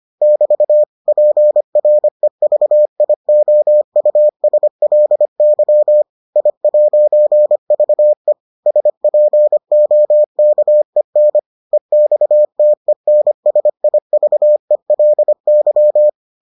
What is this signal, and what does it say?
Morse code, heard as a keyed tone, about 25 wpm, 600 Hz. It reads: <BT> PREVIOUSLY I'VE SPOKEN EXTENSIVELY